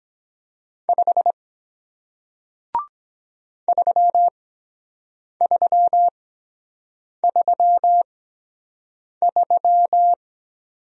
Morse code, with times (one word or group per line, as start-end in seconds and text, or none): 0.9-1.3 5
3.7-10.1 3333